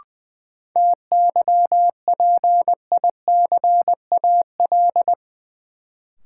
Morse code, heard as a keyed tone, about 20 wpm, 700 Hz. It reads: TYPICAL
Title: Sentences from Top 1100 Words - New Word Drill Alt - 20wpm